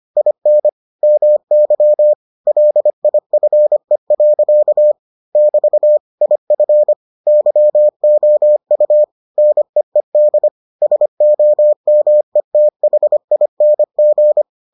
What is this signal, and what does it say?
Morse code, heard as a keyed tone, about 25 wpm, 600 Hz. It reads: IN MY LIFE. <BT> IF YOU NEED SOMETHING